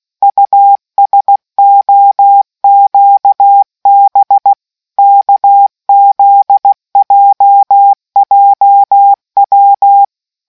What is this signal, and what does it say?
Morse code, heard as a keyed tone, about 16 wpm, 800 Hz. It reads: USOQB KZJJW